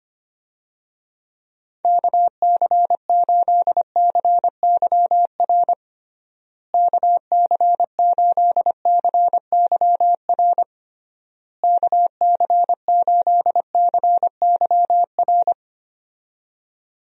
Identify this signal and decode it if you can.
Morse code, heard as a keyed tone, about 25 wpm, 700 Hz. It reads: KC8CYR KC8CYR KC8CYR
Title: USA Call Signs - 6 Characters - Repeated 3x - 25wpm